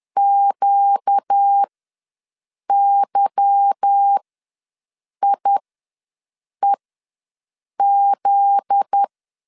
Morse code, heard as a keyed tone, about 10 wpm, spaced as for 6 wpm, 800 Hz. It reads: QYIEZ
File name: LEZIONI / 8 YJ45 / 8 YJ45 ws 10wpm 3843